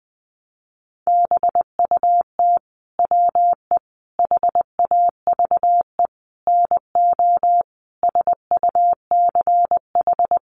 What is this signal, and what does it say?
Morse code, heard as a keyed tone, about 20 wpm, 700 Hz. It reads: BUT WE HAVE NO SUCH
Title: Sentences from Top 1300 Words 20wpm